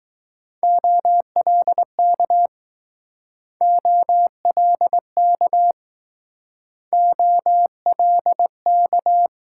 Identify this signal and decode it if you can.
Morse code, heard as a keyed tone, about 19 wpm, 700 Hz. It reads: OLK OLK OLK